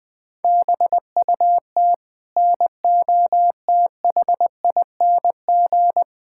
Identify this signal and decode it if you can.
Morse code, heard as a keyed tone, about 20 wpm, 700 Hz. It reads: BUT NOTHING
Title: Sets of 8 Words - 20wpm